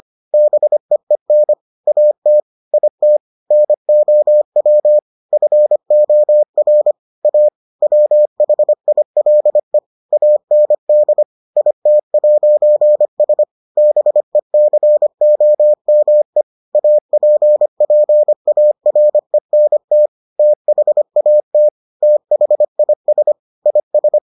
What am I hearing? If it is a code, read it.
Morse code, heard as a keyed tone, about 25 wpm, 600 Hz. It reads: BEEN AT IT NOW FOR A WHILE AND IT'S BECOME APPARENT THAT THIS IS